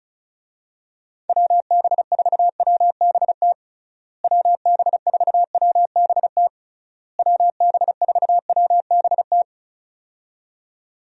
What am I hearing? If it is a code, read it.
Morse code, heard as a keyed tone, about 36 wpm, 700 Hz. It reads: WB4WBT WB4WBT WB4WBT
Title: USA Call Signs - 6 Characters - Repeated 3x - 35wpm